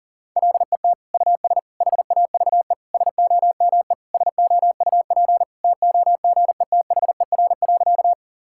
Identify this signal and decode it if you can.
Morse code, heard as a keyed tone, about 40 wpm, 700 Hz. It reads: LET US HAVE SOME SOUP TOGETHER.
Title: Sentences from Top 1000 Words - Review Part 6 - 40wpm